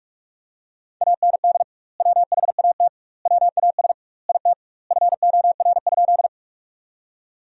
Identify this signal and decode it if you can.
Morse code, heard as a keyed tone, about 46 wpm, 700 Hz. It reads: AND WHAT WAS IT FOR?
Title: 2x Word-Spacing - Sentences from Top 100 Words - 45wpm